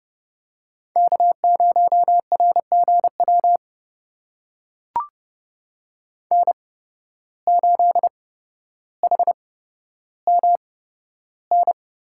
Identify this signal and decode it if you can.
Morse code, heard as a keyed tone, about 32 wpm, spaced as for 13 wpm, 700 Hz. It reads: K0RGW N8HMN
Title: USA Call Signs - 5 Characters - 8x Character Spacing - 30wpm